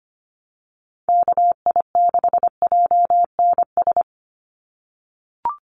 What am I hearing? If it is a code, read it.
Morse code, heard as a keyed tone, about 25 wpm, 700 Hz. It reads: KI6JNS
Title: USA Call Signs 25wpm